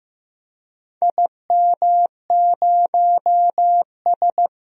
Morse code, heard as a keyed tone, about 15 wpm, 700 Hz. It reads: IM0S
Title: DXCC-100 Call Signs - 4 Characters - Repeated 2x - 15wpm